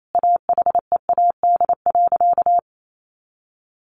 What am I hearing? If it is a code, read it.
Morse code, heard as a keyed tone, about 28 wpm, 700 Hz. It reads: AHEAD.
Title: Sentences from Top 1100 Words 28wpm